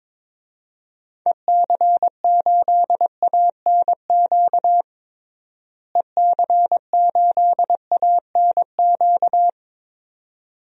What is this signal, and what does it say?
Morse code, heard as a keyed tone, about 22 wpm, 700 Hz. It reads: EC8ANQ EC8ANQ
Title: DXCC-100 Call Signs - 6 Characters - Repeated 2x - 22wpm